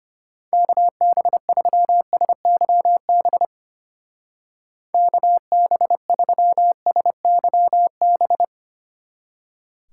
Morse code, heard as a keyed tone, about 32 wpm, 700 Hz. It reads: KB3SYB KB3SYB